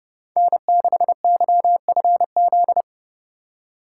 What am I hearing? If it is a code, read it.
Morse code, heard as a keyed tone, about 30 wpm, 700 Hz. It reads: N6YFZ
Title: USA Call Signs - 5 Characters - 2x Character Spacing - 30wpm